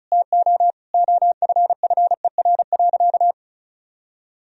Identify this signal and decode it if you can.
Morse code, heard as a keyed tone, about 34 wpm, 700 Hz. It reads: TO OFFER.
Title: Sentences from Top 1000 Words - Review Part 4 - 35wpm